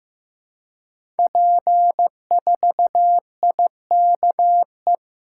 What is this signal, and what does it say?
Morse code, heard as a keyed tone, about 15 wpm, 700 Hz. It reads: P4IKE
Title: Code Groups - 5 Letter:Numbers - 15wpm